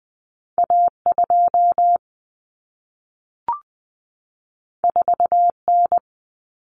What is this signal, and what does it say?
Morse code, heard as a keyed tone, about 20 wpm, 700 Hz. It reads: A2 4N
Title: Common Cut Numbers - 2-Digit 20wpm